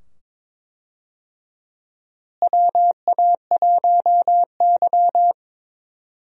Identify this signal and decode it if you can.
Morse code, heard as a keyed tone, about 22 wpm, 700 Hz. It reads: WA1Y
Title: USA Call Signs - 4 Characters - Repeated 2x - 22wpm